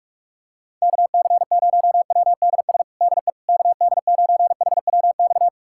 Morse code, heard as a keyed tone, about 41 wpm, 700 Hz. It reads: KC0WDS DE KD9HWX
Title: USA Callsign DE Callsign - 6 Characters - 45wpm